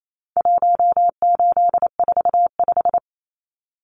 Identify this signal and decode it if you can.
Morse code, heard as a keyed tone, about 28 wpm, 700 Hz. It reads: 1845
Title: Numbers - 4-Digit 28wpm